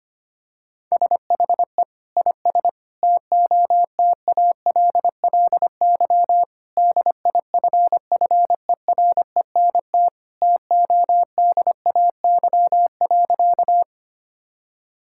Morse code, heard as a keyed tone, about 25 wpm, 700 Hz. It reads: SHE IS TOTALLY DIFFERENT TODAY.
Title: Sentences from Top 1600 Words 25wpm